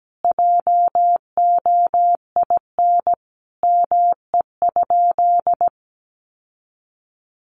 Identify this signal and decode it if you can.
Morse code, heard as a keyed tone, about 17 wpm, 700 Hz. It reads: JOIN ME?
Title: Sentences from Top 1400 Words 17wpm